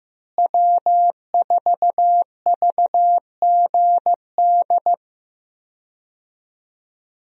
Morse code, heard as a keyed tone, about 15 wpm, 700 Hz. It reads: W4VGD
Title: USA Call Signs - 5 Characters - Repeated 3x - 15wpm